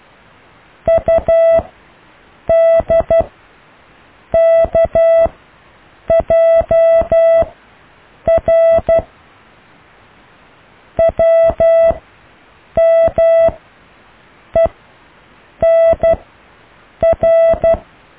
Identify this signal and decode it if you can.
Morse code, heard as a keyed tone, about 12 wpm, spaced as for 7 wpm, 660 Hz. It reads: UDKJR WMENR
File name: cw / IJLUM+CDRO+EKTPV+BFHNW